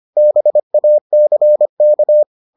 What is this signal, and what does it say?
Morse code, heard as a keyed tone, about 25 wpm, 600 Hz. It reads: BACK